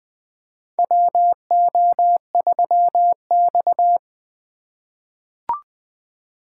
Morse code, heard as a keyed tone, about 20 wpm, 700 Hz. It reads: WO3X
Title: USA Call Signs - 4 Characters - 3x Character Spacing - 20wpm